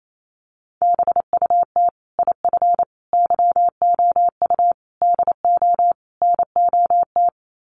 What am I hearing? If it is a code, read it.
Morse code, heard as a keyed tone, about 28 wpm, 700 Hz. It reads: BUT IF YOU DO NOT
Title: Sets of 5 Words Encore 28wpm